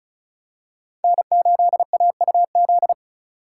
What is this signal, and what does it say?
Morse code, heard as a keyed tone, about 36 wpm, 700 Hz. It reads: N8AUZ
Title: USA Call Signs - 4x Character Spacing - 35wpm